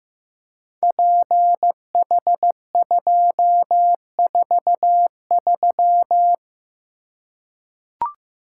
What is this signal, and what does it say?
Morse code, heard as a keyed tone, about 15 wpm, 700 Hz. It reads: PH243
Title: Code Groups - 5 Letter:Numbers - 15wpm